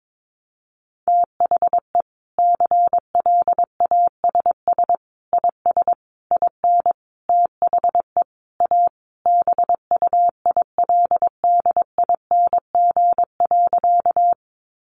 Morse code, heard as a keyed tone, about 22 wpm, 700 Hz. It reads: THE CLASS IS IN THE A BUILDING.
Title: Additional Easy Phrases 22wpm